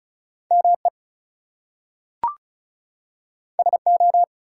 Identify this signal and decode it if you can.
Morse code, heard as a keyed tone, about 36 wpm, 700 Hz. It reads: ME SO
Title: Two Letter Words - 35wpm